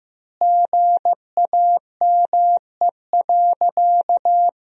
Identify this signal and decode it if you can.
Morse code, heard as a keyed tone, about 15 wpm, 700 Hz. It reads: GAME.
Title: 2x Word-Spacing - Sentences from Top 500 Words - 15wpm